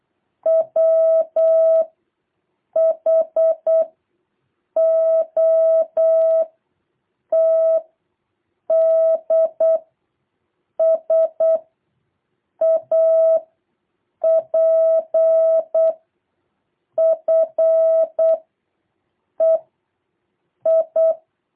Morse code, heard as a keyed tone, about 8 wpm, 650 Hz. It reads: W H O T D S A P F E I
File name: jh8cbh / 2023cw